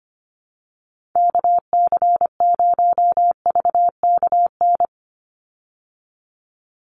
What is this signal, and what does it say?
Morse code, heard as a keyed tone, about 25 wpm, 700 Hz. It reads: KC0VKN
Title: Call Signs Repeated 3x - 25wpm